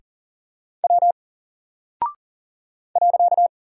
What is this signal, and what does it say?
Morse code, heard as a keyed tone, about 40 wpm, 700 Hz. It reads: W .
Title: Single Character (Letter:Number:Punctuation:Prosign) Alt 40wpm